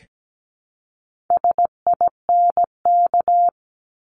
Morse code, heard as a keyed tone, about 17 wpm, 700 Hz. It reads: SINK
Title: Top 1751-2000 Words 17wpm